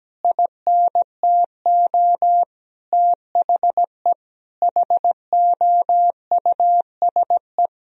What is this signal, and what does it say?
Morse code, heard as a keyed tone, about 17 wpm, 700 Hz. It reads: INTO THE HOUSE
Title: Sentences from Top 1000 Words - Review Part 1 - 17wpm